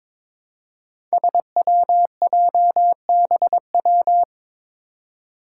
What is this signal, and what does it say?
Morse code, heard as a keyed tone, about 22 wpm, 700 Hz. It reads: SWJBW
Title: Five Challenging Letter-Number Combinations 22wpm